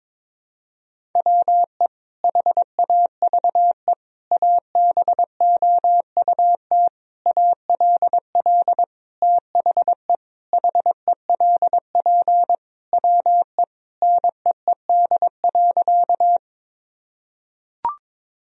Morse code, heard as a keyed tone, about 21 wpm, 700 Hz. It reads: WE HAVE ABOUT ALL THE HELP WE NEED.
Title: Sentences from Top 1000 Words - Review Part 1 - 22wpm